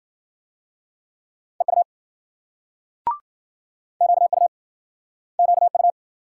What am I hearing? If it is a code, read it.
Morse code, heard as a keyed tone, about 53 wpm, 700 Hz. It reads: EU QU QU